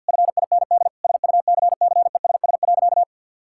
Morse code, heard as a keyed tone, about 50 wpm, 700 Hz. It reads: FIND SUCCESS.
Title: Sentences from Top 1000 Words - Review Part 6 - 50wpm